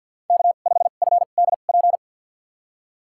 Morse code, heard as a keyed tone, about 53 wpm, spaced as for 37 wpm, 700 Hz. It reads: K5FDP
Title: USA Call Signs - 5 Characters - 2x Character Spacing - 50wpm